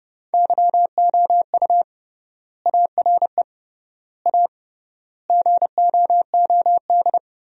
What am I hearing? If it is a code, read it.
Morse code, heard as a keyed tone, about 30 wpm, 700 Hz. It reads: YOU ARE A GOOD
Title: 3x Word-Spacing - Sentences from Top 200 Words - 30wpm